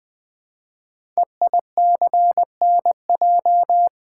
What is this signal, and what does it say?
Morse code, heard as a keyed tone, about 20 wpm, 700 Hz. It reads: EICNJ
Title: Code Groups - 5 Letters - 20wpm